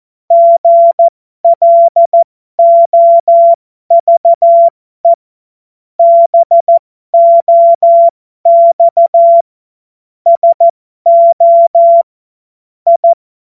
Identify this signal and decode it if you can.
Morse code, heard as a keyed tone, about 14 wpm, spaced as for 12 wpm, 680 Hz. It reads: GLOVE BOX SO I